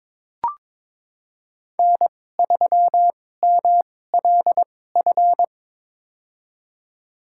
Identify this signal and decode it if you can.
Morse code, heard as a keyed tone, about 22 wpm, spaced as for 16 wpm, 700 Hz. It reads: N3MLF
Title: USA Call Signs - 5 Characters - 2x Character Spacing - 22wpm